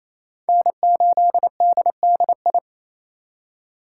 Morse code, heard as a keyed tone, about 34 wpm, 700 Hz. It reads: N8DDI
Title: Speed Racing - USA Call Signs 28wpm